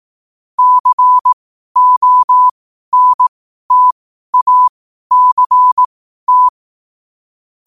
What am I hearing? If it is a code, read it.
Morse code, heard as a keyed tone, about 18 wpm, spaced as for 12 wpm, 1000 Hz. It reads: CONTACT
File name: qotdfiles / 12wpm